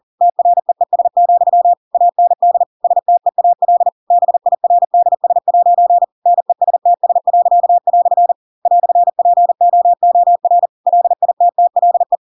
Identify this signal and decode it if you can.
Morse code, heard as a keyed tone, about 40 wpm, 700 Hz. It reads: TREES, AND STEAL BIRDS' NESTS." "POOR LITTLE
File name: Pinocchio / Pinocchio ch4 40 40 ik1hge